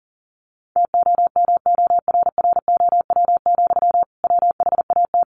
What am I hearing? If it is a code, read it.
Morse code, heard as a keyed tone, about 40 wpm, 700 Hz. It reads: TOMORROW, WHAT
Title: Sentences from Top 1500 Words 40wpm